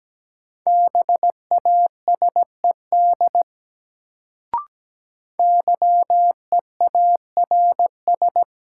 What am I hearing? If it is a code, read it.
Morse code, heard as a keyed tone, about 17 wpm, 700 Hz. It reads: BASED YEARS
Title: Sets of 2 Words - New Word Drill - 17wpm